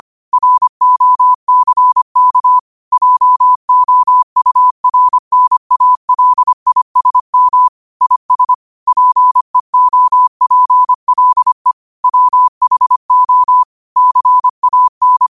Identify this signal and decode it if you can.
Morse code, heard as a keyed tone, about 25 wpm, 1000 Hz. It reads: ROCK JOURNALISM IS PEOPLE WHO CAN